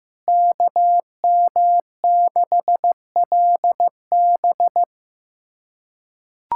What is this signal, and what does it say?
Morse code, heard as a keyed tone, about 15 wpm, 700 Hz. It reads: KM6LB
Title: USA Callsign DE Callsign 15wpm